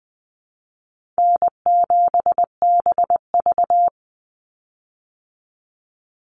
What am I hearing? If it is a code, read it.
Morse code, heard as a keyed tone, about 20 wpm, 700 Hz. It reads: N7BV
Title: Call Signs Repeated 2x - 20wpm